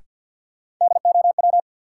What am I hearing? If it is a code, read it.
Morse code, heard as a keyed tone, about 52 wpm, 700 Hz. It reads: DOW
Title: Sets of 5 Words - New Word Drill - 50wpm